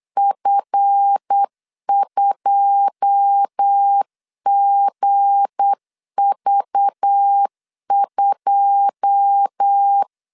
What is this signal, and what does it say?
Morse code, heard as a keyed tone, about 8 wpm, 800 Hz. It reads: F2GV2
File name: LEZIONI / 9 2 8wpm 3437